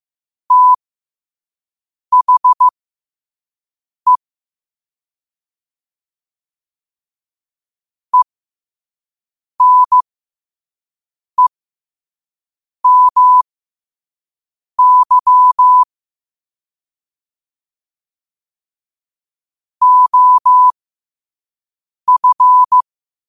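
Morse code, heard as a keyed tone, about 15 wpm, spaced as for 5 wpm, 1000 Hz. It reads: THE ENEMY OF
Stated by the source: Morse quote of the day at 5 wpm.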